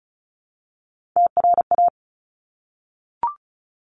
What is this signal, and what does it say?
Morse code, heard as a keyed tone, about 35 wpm, 700 Hz. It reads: TRA
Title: Top Three Letter Combinations in Words 35wpm